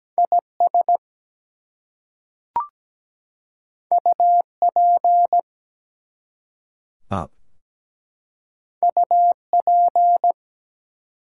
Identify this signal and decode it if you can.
Morse code, heard as a keyed tone, about 17 wpm, 700 Hz. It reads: IS UP UP
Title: Two Letter Words - 17wpm